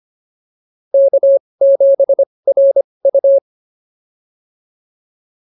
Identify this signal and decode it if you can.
Morse code, heard as a keyed tone, about 25 wpm, spaced as for 20 wpm, 550 Hz. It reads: K7RU
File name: POTA206 20